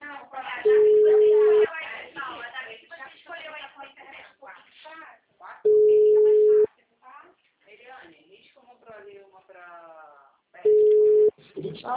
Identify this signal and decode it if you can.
Ringback tone (Europe, 425 Hz), with speech.